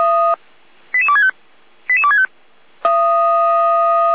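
Five-tone selective call (ZVEI), heard as quick runs of five tones, tones of about 70 ms. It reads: 88265 88265
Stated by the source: ZVEI five-tone call received by SDR.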